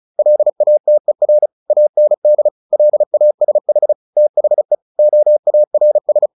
Morse code, heard as a keyed tone, about 34 wpm, 600 Hz. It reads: LATER AND LASH THE OARS
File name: the old man and the sea 2 35wpm